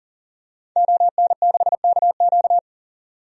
Morse code, heard as a keyed tone, about 38 wpm, 700 Hz. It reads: ON6KQ
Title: DXCC-100 Call Signs - 5 Characters - 40wpm